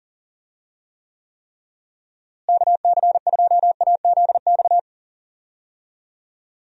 Morse code, heard as a keyed tone, about 38 wpm, 700 Hz. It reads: KC2AZX
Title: USA Call Signs - 6 Characters - Repeated 3x - 40wpm